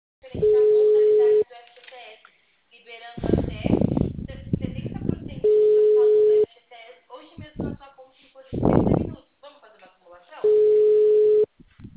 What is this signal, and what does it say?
Ringback tone (Europe, 425 Hz), with speech.
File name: out- -2013- - - .9904